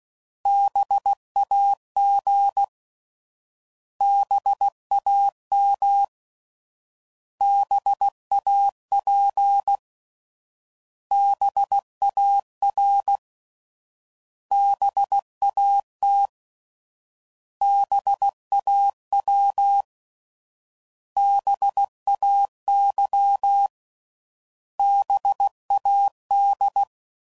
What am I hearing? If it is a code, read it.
Morse code, heard as a keyed tone, about 16 wpm, 800 Hz. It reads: BAG BAM BAP BAR BAT BAW BAY BAD